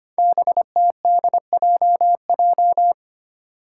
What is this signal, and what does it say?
Morse code, heard as a keyed tone, about 25 wpm, 700 Hz. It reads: BTDJJ
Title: Five Challenging Letter-Number Combinations 25wpm